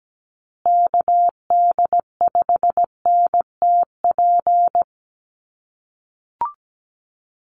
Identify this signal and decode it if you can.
Morse code, heard as a keyed tone, about 17 wpm, 700 Hz. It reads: KD5NTP
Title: USA Call Signs Repeated 2x 17wpm